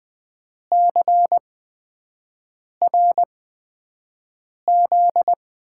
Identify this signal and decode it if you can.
Morse code, heard as a keyed tone, about 20 wpm, spaced as for 5 wpm, 700 Hz. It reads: CRZ
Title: USA Call Signs - 8x Character Spacing - 20wpm